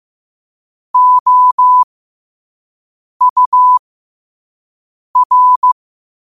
Morse code, heard as a keyed tone, about 15 wpm, spaced as for 5 wpm, 1000 Hz. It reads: OUR
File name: qotdfiles / 5wpm